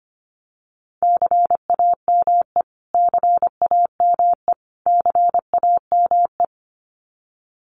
Morse code, heard as a keyed tone, about 25 wpm, 700 Hz. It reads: CAME CAME CAME
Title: Top 200 Words Repeated 25wpm